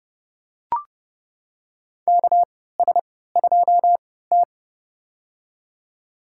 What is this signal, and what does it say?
Morse code, heard as a keyed tone, about 30 wpm, spaced as for 17 wpm, 700 Hz. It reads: KS2T
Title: USA Call Signs - 4 Characters - 3x Character Spacing - 30wpm